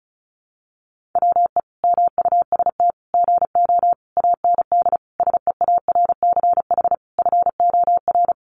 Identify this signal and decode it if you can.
Morse code, heard as a keyed tone, about 34 wpm, 700 Hz. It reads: WE MUST GO AND SEARCH FOR
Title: Sentences from Top 800 Words 35wpm